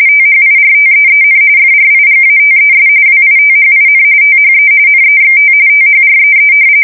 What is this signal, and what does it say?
RTTY, about 45.45 baud. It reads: THIRTEEN FRIGATES WITHIN THE NEXT THREE
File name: fsk170 / usnavy1